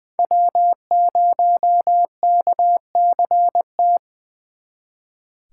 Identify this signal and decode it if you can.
Morse code, heard as a keyed tone, about 24 wpm, 700 Hz. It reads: W0KCT
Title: Speed Racing - USA 1x3 Call Signs 20wpm